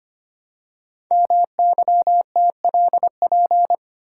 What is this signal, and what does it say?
Morse code, heard as a keyed tone, about 25 wpm, 700 Hz. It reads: MYTLP
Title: Code Groups - 5 Letters - 25wpm